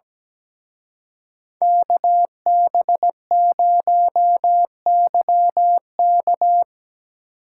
Morse code, heard as a keyed tone, about 18 wpm, 700 Hz. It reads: KB0YK
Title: Speed Racing - USA 2x2 Call Signs 17wpm